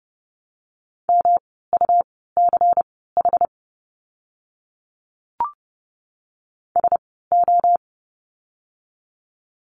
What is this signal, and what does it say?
Morse code, heard as a keyed tone, about 30 wpm, spaced as for 17 wpm, 700 Hz. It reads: MUCH SO
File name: course / 30:10wpm / 115 Lesson 19f - Letter w - Words 30wpm